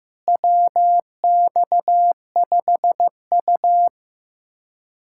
Morse code, heard as a keyed tone, about 15 wpm, 700 Hz. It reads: WX5U
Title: USA Callsign DE Callsign - 4 Characters - 15wpm